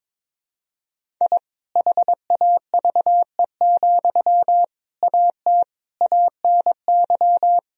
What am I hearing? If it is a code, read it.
Morse code, heard as a keyed tone, about 22 wpm, 700 Hz. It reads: I HAVE, AT ANY
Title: Sentences from Top 1000 Words - Review Part 4 - 22wpm